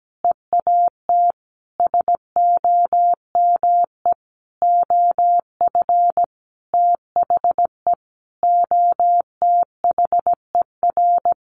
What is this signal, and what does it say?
Morse code, heard as a keyed tone, about 17 wpm, 700 Hz. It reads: EAT SOME OF THE OTHER
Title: Sentences from Top 900 Words 17wpm